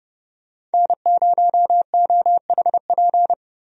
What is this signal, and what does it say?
Morse code, heard as a keyed tone, about 30 wpm, 700 Hz. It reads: N0OHP
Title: USA Call Signs - 5 Characters - Repeated 3x - 30wpm